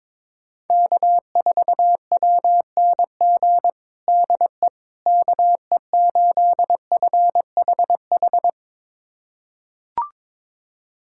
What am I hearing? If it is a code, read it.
Morse code, heard as a keyed tone, about 22 wpm, 700 Hz. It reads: K4WNG DE KE8FHH
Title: USA Callsign DE Callsign 22wpm